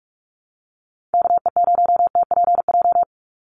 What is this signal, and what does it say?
Morse code, heard as a keyed tone, about 44 wpm, 700 Hz. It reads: KE0TPJ
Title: USA Call Signs 45wpm